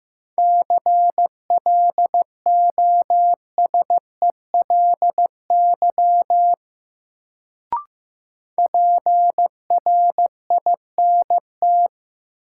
Morse code, heard as a keyed tone, about 15 wpm, 700 Hz. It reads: CLOSELY PRINT
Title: Sentences from Top 1400 Words - New Word Drill - 15wpm